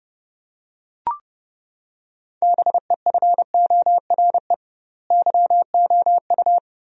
Morse code, heard as a keyed tone, about 32 wpm, 700 Hz. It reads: BEFORE YOU
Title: 2x Word-Spacing - Sentences from Top 400 Words - 30wpm